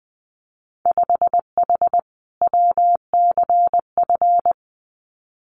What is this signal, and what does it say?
Morse code, heard as a keyed tone, about 20 wpm, 700 Hz. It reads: 5H WCF
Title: Winter Field Day Exchanges Encore 20wpm